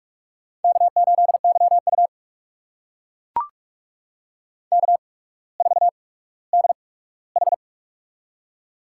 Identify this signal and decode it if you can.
Morse code, heard as a keyed tone, about 44 wpm, spaced as for 19 wpm, 700 Hz. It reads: K8YU K4DH
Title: USA Call Signs - 4 Characters - 8x Character Spacing - 45wpm